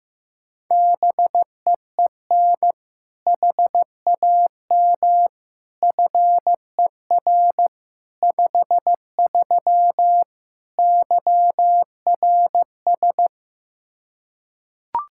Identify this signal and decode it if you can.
Morse code, heard as a keyed tone, about 15 wpm, 700 Hz. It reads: BEEN HAM FER 53 YRS
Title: Practice QSO - Exchange 3 - More Info - 15wpm